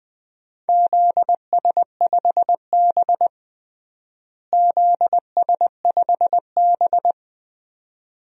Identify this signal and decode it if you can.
Morse code, heard as a keyed tone, about 21 wpm, 700 Hz. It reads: ZS5B ZS5B
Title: DXCC-100 Call Signs - 4 Characters - Repeated 2x - 20wpm